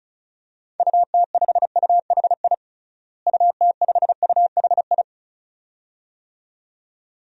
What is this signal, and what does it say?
Morse code, heard as a keyed tone, about 36 wpm, 700 Hz. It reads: UT5UHI UT5UHI
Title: DXCC-100 Call Signs - 6 Characters - Repeated 2x - 35wpm